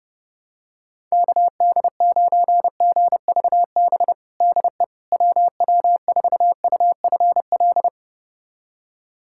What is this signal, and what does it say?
Morse code, heard as a keyed tone, about 30 wpm, 700 Hz. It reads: KD9GVB DE WW4UFL
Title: USA Callsign DE Callsign - 6 Characters - 30wpm